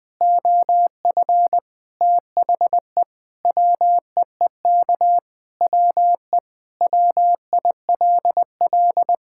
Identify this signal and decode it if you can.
Morse code, heard as a keyed tone, about 20 wpm, 700 Hz. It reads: OF THE WEEK WE WILL
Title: Sentences with 5-Word Phrases - 20wpm